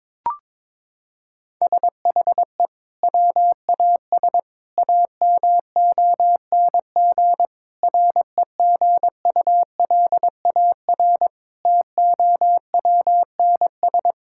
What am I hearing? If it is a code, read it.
Morse code, heard as a keyed tone, about 22 wpm, 700 Hz. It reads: SHE WAS AMONG REGULAR TOWNS
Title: Sentences from Top 1000 Words - Review Part 6 - 22wpm